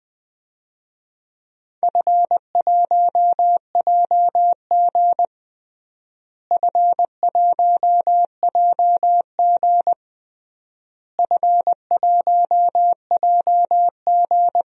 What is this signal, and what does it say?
Morse code, heard as a keyed tone, about 21 wpm, 700 Hz. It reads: F1JG F1JG F1JG
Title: DXCC-100 Call Signs - 4 Characters - Repeated 3x - 20wpm